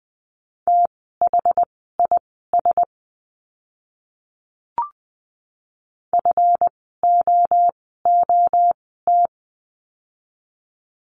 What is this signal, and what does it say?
Morse code, heard as a keyed tone, about 20 wpm, spaced as for 14 wpm, 700 Hz. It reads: THIS FOOT
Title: 139 Lesson 23f - Letter f - Words 20wpm